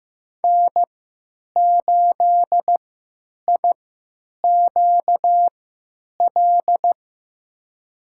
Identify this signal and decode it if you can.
Morse code, heard as a keyed tone, about 15 wpm, spaced as for 9 wpm, 700 Hz. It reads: N8IQL
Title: USA Call Signs - 3x Character Spacing - 15wpm